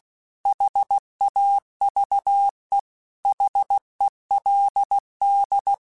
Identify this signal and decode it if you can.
Morse code, heard as a keyed tone, about 16 wpm, 800 Hz. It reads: HAVE HELD
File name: QSOs / QSOs-mp3s / kc7sac@15